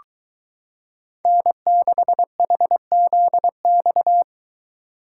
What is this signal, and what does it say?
Morse code, heard as a keyed tone, about 23 wpm, 700 Hz. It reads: N6HZX